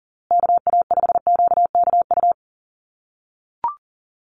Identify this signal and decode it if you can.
Morse code, heard as a keyed tone, about 40 wpm, 700 Hz. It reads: KA5QKU
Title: USA Call Signs Repeated 2x 40wpm